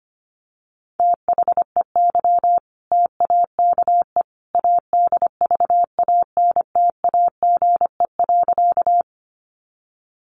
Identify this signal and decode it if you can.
Morse code, heard as a keyed tone, about 25 wpm, 700 Hz. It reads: THEY TAKE ADVANTAGE.
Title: Sentences from Top 900 Words 25wpm